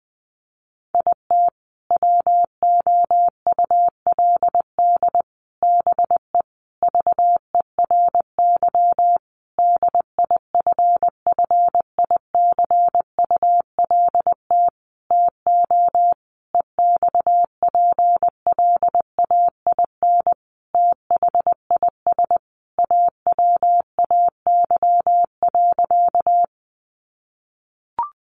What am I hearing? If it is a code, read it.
Morse code, heard as a keyed tone, about 20 wpm, 700 Hz. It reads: IT WOULD BE VERY DIFFICULT TO EXPLAIN THIS AWAY.
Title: Sentences from Top 1100 Words 20wpm